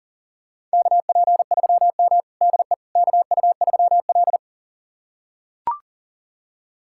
Morse code, heard as a keyed tone, about 40 wpm, 700 Hz. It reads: KP3M DE KU3L
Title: USA Callsign DE Callsign - 4 Characters - 40wpm